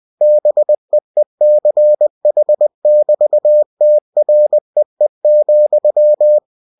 Morse code, heard as a keyed tone, about 20 wpm, 600 Hz. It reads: BEECH<BT>TREE,
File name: cw-books / winnie-the-pooh / winnie-20wpm-page-0020